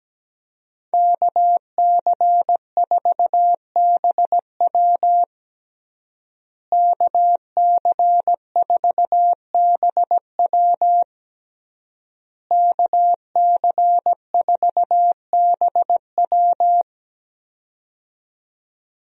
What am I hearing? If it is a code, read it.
Morse code, heard as a keyed tone, about 17 wpm, 700 Hz. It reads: KC4BW KC4BW KC4BW
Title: USA Call Signs - 5 Characters - Repeated 3x - 17wpm